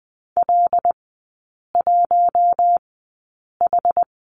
Morse code, heard as a keyed tone, about 20 wpm, spaced as for 8 wpm, 700 Hz. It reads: L1H
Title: 074 Lesson 13e - Letter L - Three Characters 20wpm